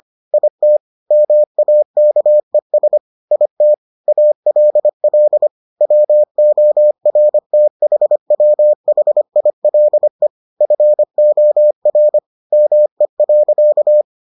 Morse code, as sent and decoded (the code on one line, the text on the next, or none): .. - / -- .- -.- . ... / .. - / .- .-.. .-.. / .-- --- .-. - .... .-- .... .. .-.. / ..-. --- .-. / -- . .-.-.-
IT MAKES IT ALL WORTHWHILE FOR ME.